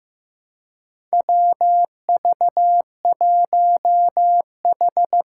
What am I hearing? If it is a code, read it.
Morse code, heard as a keyed tone, about 15 wpm, 700 Hz. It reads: WV1H
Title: Four Challenging Letter-Number Combinations 15wpm